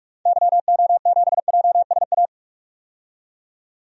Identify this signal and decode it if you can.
Morse code, heard as a keyed tone, about 42 wpm, 700 Hz. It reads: YO7JSA
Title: DXCC-100 Call Signs - 6 Characters - 45wpm